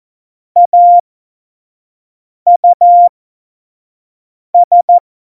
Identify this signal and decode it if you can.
Morse code, heard as a keyed tone, about 14 wpm, spaced as for 5 wpm, 700 Hz. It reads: AUS
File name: Morsekurs / Witze / Witz 21. November 2017, 25 F70